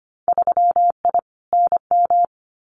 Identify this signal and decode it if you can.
Morse code, heard as a keyed tone, about 25 wpm, 700 Hz. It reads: 3I NM
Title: Winter Field Day Exchanges Encore 25wpm